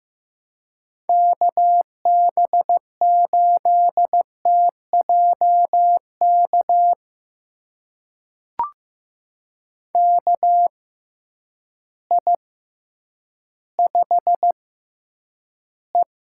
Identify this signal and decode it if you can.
Morse code, heard as a keyed tone, about 15 wpm, spaced as for 9 wpm, 700 Hz. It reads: KB8TJK KI5E
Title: USA Call Signs - 6 Characters - 6x Character Spacing - 15wpm